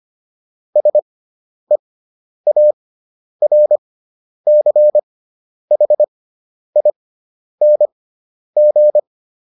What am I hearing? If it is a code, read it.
Morse code, heard as a keyed tone, about 25 wpm, spaced as for 10 wpm, 600 Hz. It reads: SEARCHING